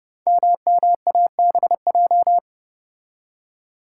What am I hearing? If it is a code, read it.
Morse code, heard as a keyed tone, about 30 wpm, 700 Hz. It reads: MMABJ
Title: Code Groups - 5 Letter:Numbers - 30wpm